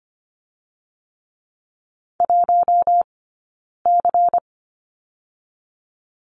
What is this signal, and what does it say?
Morse code, heard as a keyed tone, about 25 wpm, 700 Hz. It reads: 1 C
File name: course / 25:10wpm / 092 Lesson 16d - Letter c - Two Characters 25wpm